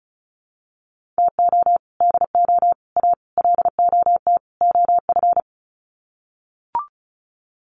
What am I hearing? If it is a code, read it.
Morse code, heard as a keyed tone, about 34 wpm, 700 Hz. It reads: TO DO A LOT OF
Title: Sets of 5 Words Encore 35wpm